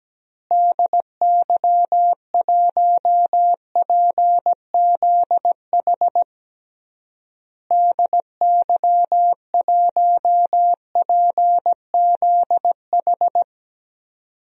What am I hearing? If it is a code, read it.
Morse code, heard as a keyed tone, about 17 wpm, 700 Hz. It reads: DY1PZH DY1PZH
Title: DXCC-100 Call Signs - 6 Characters - Repeated 3x - 17wpm